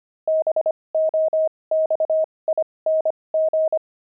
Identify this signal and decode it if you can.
Morse code, heard as a keyed tone, about 25 wpm, spaced as for 20 wpm, 630 Hz. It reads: BOXING
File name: pangram-fs-20-cs-25-jumping